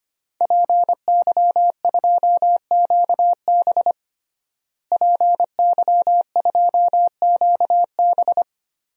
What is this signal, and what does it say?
Morse code, heard as a keyed tone, about 25 wpm, 700 Hz. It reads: PY2QB PY2QB
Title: DXCC-100 Call Signs - 5 Characters - Repeated 2x - 25wpm